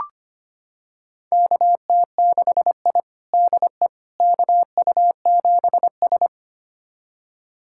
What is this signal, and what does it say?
Morse code, heard as a keyed tone, about 25 wpm, 700 Hz. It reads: KT6I DE KU7S
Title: USA Callsign DE Callsign - 4 Characters - 25wpm